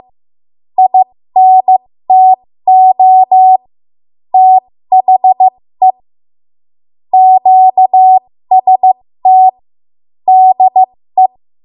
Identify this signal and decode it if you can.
Morse code, heard as a keyed tone, about 15 wpm, spaced as for 13 wpm, 750 Hz. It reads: INTO THE QST DE
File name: Morse / 13WPM